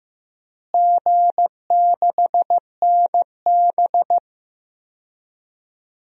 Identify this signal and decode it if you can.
Morse code, heard as a keyed tone, about 15 wpm, 700 Hz. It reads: G6NB
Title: DXCC-100 Call Signs - 4 Characters - Repeated 2x - 15wpm